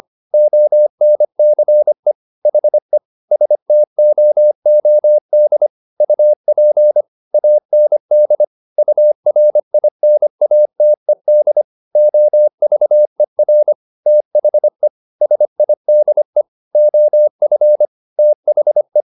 Morse code, heard as a keyed tone, about 25 wpm, 600 Hz. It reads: ONCE HE STOOD UP AND URINATED OVER THE SIDE OF THE
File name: the old man and the sea part 1 25wpm